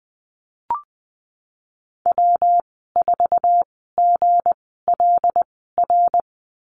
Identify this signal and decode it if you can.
Morse code, heard as a keyed tone, about 20 wpm, spaced as for 15 wpm, 700 Hz. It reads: W4GLR
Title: 200 Lesson 32g - Letter k - Call Signs 20wpm